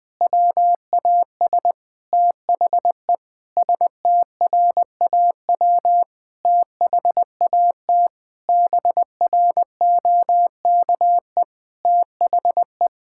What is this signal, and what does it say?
Morse code, heard as a keyed tone, about 20 wpm, 700 Hz. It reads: WAS THE STRAW THAT BROKE THE
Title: Sentences with 8-Word Phrases - 20wpm